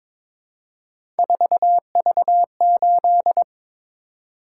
Morse code, heard as a keyed tone, about 23 wpm, 700 Hz. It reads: 4V8
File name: practice / 22wpm / Three Letter-Number Combinations - 22wpm